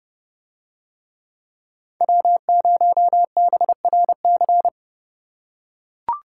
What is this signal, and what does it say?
Morse code, heard as a keyed tone, about 30 wpm, 700 Hz. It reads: W0BRC